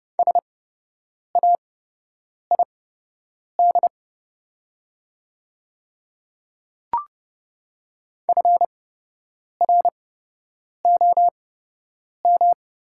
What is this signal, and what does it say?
Morse code, heard as a keyed tone, about 30 wpm, spaced as for 8 wpm, 700 Hz. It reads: SAID FROM
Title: 8x Character Spacing - Top 100 Words - 30wpm